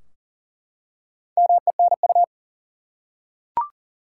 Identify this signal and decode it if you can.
Morse code, heard as a keyed tone, about 40 wpm, 700 Hz. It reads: MENU
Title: Four Letter Words - 40wpm